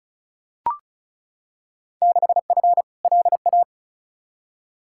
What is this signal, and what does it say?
Morse code, heard as a keyed tone, about 36 wpm, 700 Hz. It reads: 6F LA
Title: Field Day Exchanges Encore - 35wpm